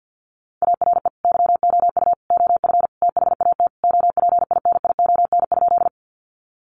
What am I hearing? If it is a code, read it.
Morse code, heard as a keyed tone, about 50 wpm, 700 Hz. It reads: ARE YOU OF THAT OPINION?
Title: Sentences from Top 1200 Words 50wpm